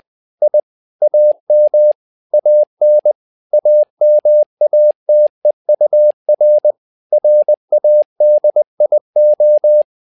Morse code, heard as a keyed tone, about 20 wpm, 600 Hz. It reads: I AM AN AMATEUR RADIO
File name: tower trouble 20wpm